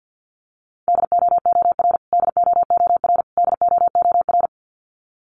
Morse code, heard as a keyed tone, about 52 wpm, 700 Hz. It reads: DOOR DOOR DOOR
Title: Top 300 Words Repeated 50wpm